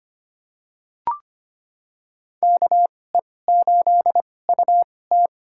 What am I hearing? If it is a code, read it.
Morse code, heard as a keyed tone, about 25 wpm, spaced as for 18 wpm, 700 Hz. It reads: KE8UT
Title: USA Call Signs - 5 Characters - 2x Character Spacing - 25wpm